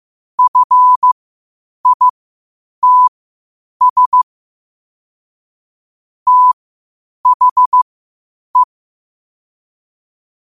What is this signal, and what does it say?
Morse code, heard as a keyed tone, about 15 wpm, spaced as for 8 wpm, 1000 Hz. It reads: FITS THE
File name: qotdfiles / 8wpm